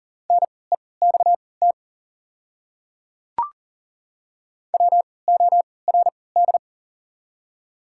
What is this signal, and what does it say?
Morse code, heard as a keyed tone, about 40 wpm, spaced as for 23 wpm, 700 Hz. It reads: NEXT WORD
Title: 3x Character Spacing - Top 300 Words - 40wpm